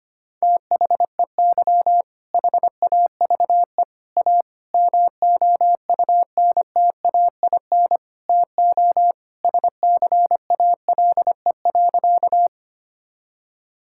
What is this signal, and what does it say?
Morse code, heard as a keyed tone, about 25 wpm, 700 Hz. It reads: THEY HAVE A MOUNTAIN TO SCALE.
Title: Sentences from Top 1000 Words - Review Part 5 - 25wpm